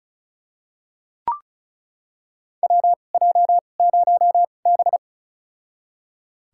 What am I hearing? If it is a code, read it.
Morse code, heard as a keyed tone, about 36 wpm, spaced as for 25 wpm, 700 Hz. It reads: WJ0B
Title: USA Call Signs - 4 Characters - 2x Character Spacing - 35wpm